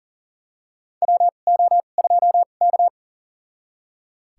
Morse code, heard as a keyed tone, about 40 wpm, spaced as for 29 wpm, 690 Hz. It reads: WO2K
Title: USA Call Signs - 4 Characters - 2x Character Spacing - 40wpm